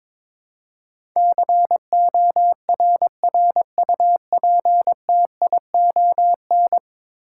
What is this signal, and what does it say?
Morse code, heard as a keyed tone, about 22 wpm, 700 Hz. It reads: CORRUPTION
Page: morse-code-ninja-practice-22wpm